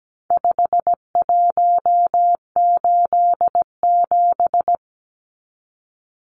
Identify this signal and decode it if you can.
Morse code, heard as a keyed tone, about 17 wpm, 700 Hz. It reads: 5187
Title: Numbers - 4-Digit 17wpm